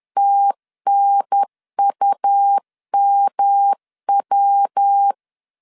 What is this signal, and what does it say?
Morse code, heard as a keyed tone, about 10 wpm, 800 Hz. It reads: TNUMW
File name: CW / DET / 1 3 1500C S B 10wpm 3335